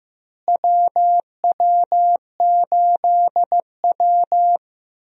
Morse code, heard as a keyed tone, about 15 wpm, 700 Hz. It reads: WW8W
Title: USA Call Signs - 4 Characters - Repeated 3x - 15wpm